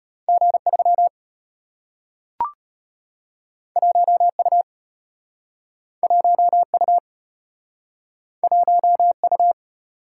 Morse code, heard as a keyed tone, about 35 wpm, 700 Hz. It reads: G3 1U 1U 1U